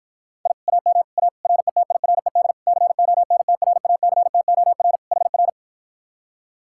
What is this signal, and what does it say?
Morse code, heard as a keyed tone, about 53 wpm, 700 Hz. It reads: I AM A RETIRED CONTRACTOR HR